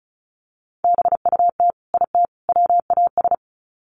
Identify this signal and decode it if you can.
Morse code, heard as a keyed tone, about 34 wpm, 700 Hz. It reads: BUT IT WAS
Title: Sets of 3 Words 35wpm